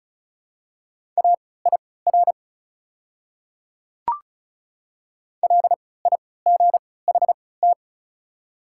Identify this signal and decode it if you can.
Morse code, heard as a keyed tone, about 36 wpm, spaced as for 20 wpm, 700 Hz. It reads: AIR LIGHT
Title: 3x Character Spacing - Top 300 Words - 35wpm